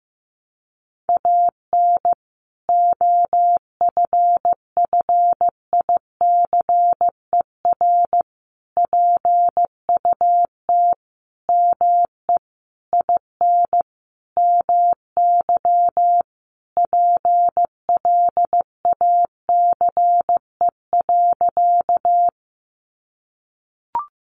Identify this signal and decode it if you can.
Morse code, heard as a keyed tone, about 15 wpm, 700 Hz. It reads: AN OFFICER PUT ME IN MY PLACE.
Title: Sentences from Top 1200 Words 15wpm